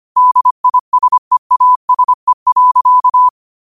Morse code, heard as a keyed tone, about 25 wpm, 1000 Hz. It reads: DISEASE.
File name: qotdfiles / 25wpm